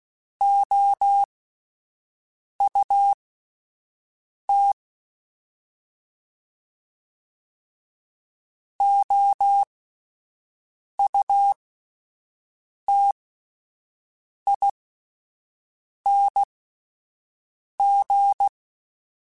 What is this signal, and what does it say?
Morse code, heard as a keyed tone, about 16 wpm, spaced as for 5 wpm, 800 Hz. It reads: OUT OUTING
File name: Alphabetical / Alphabetical mp3s / Os @5